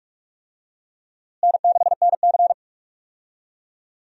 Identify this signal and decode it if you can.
Morse code, heard as a keyed tone, about 46 wpm, 700 Hz. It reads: N6NC
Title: USA Call Signs - 4 Characters - Repeated 3x - 45wpm